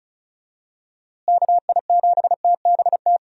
Morse code, heard as a keyed tone, about 36 wpm, 700 Hz. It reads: KI7TBT
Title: USA Call Signs - 6 Characters - Repeated 3x - 35wpm